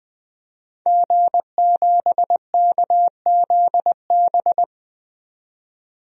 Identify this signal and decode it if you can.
Morse code, heard as a keyed tone, about 20 wpm, 700 Hz. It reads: G7KZB
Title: DXCC-100 Call Signs - 5 Characters - 20wpm